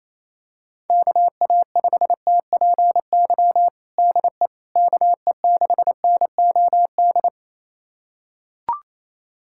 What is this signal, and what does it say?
Morse code, heard as a keyed tone, about 27 wpm, 700 Hz. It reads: KA5TPY DE KE6NOD
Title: USA Callsign DE Callsign - 6 Characters - 28wpm